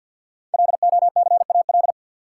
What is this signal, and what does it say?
Morse code, heard as a keyed tone, about 51 wpm, 700 Hz. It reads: LOCAL
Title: Sentences from Top 1500 Words - New Word Drill Alt - 50wpm